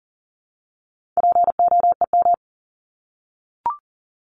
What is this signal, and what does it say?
Morse code, heard as a keyed tone, about 40 wpm, 700 Hz. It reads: POEM
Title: Top 1751-2000 Words 40wpm